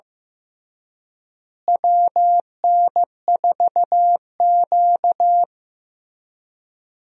Morse code, heard as a keyed tone, about 15 wpm, 700 Hz. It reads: WN4Q
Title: USA Call Signs - 4 Characters - Repeated 3x - 15wpm